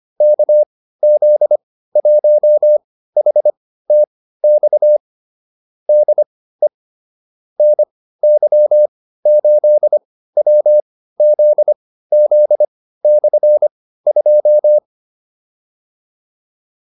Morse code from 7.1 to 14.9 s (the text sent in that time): NY8WZZ/2